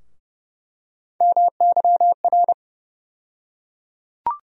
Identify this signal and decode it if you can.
Morse code, heard as a keyed tone, about 31 wpm, 700 Hz. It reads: MYR
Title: USA Call Sign Postfixes - 3 Letters 30wpm